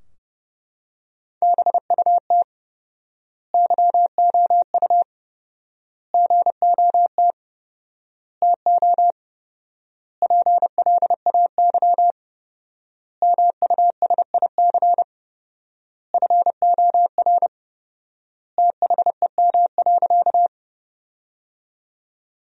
Morse code, heard as a keyed tone, about 32 wpm, 700 Hz. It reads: BUT YOU GOT TO PLAY MUSIC FOR THEM.
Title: 4x Word-Spacing - Sentences from Top 300 Words - 30wpm